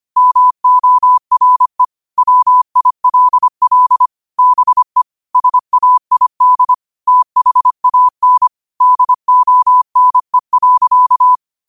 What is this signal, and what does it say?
Morse code, heard as a keyed tone, about 25 wpm, 1000 Hz. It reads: MORE WILL BE SAID THAN DONE.
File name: qotdfiles / 25wpm